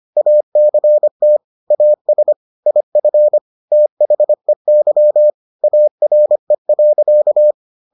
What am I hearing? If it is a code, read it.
Morse code, heard as a keyed tone, about 25 wpm, 600 Hz. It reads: ACT AS IF THEY ARE.